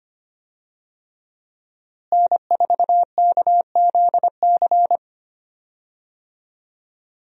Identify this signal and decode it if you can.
Morse code, heard as a keyed tone, about 25 wpm, 700 Hz. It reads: N4KZC
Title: USA Call Signs - 5 Characters - Repeated 3x - 25wpm